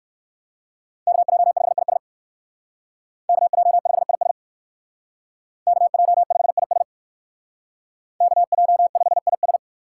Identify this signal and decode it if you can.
Morse code, heard as a keyed tone, about 58 wpm, 700 Hz. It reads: KJ5IS KJ5IS KJ5IS KJ5IS